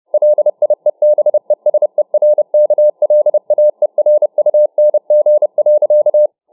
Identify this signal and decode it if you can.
Morse code, heard as a keyed tone, about 30 wpm, 600 Hz. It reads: LIEBESERKLAERUNG.